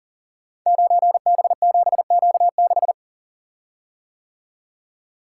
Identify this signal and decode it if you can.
Morse code, heard as a keyed tone, about 40 wpm, 700 Hz. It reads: 9B7Q6
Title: Code Groups - 5 Letter:Numbers - 40wpm